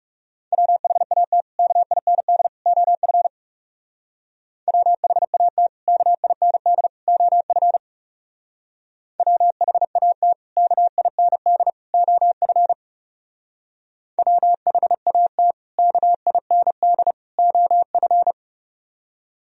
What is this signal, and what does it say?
Morse code, heard as a keyed tone, about 35 wpm, 700 Hz. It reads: WHAT KIND OF WHAT KIND OF WHAT KIND OF WHAT KIND OF
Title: Speed-Racing - Sets of 3 Words 30wpm